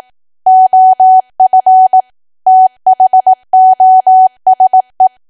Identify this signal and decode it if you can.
Morse code, heard as a keyed tone, about 18 wpm, 750 Hz. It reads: OF THOSE
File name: cwa / W1AW-Level2 / 18WPM